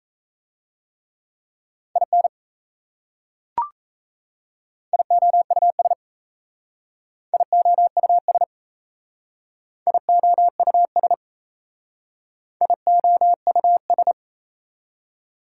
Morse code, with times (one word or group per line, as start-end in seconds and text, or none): 2.0-2.3 IN
4.9-5.9 IOUS
7.3-8.4 IOUS
9.9-11.1 IOUS
12.6-14.1 IOUS